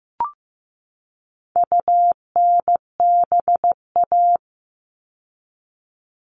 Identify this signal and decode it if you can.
Morse code, heard as a keyed tone, about 15 wpm, 700 Hz. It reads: UNBA
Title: All Cut Numbers - 4-Digit 15wpm